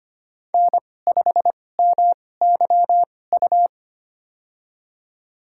Morse code, heard as a keyed tone, about 25 wpm, spaced as for 18 wpm, 700 Hz. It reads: N5MYU
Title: USA Call Signs - 5 Characters - 2x Character Spacing - 25wpm